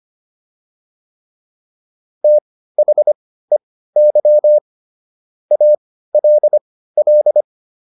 Morse code, heard as a keyed tone, about 25 wpm, spaced as for 15 wpm, 600 Hz. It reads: THEY ALL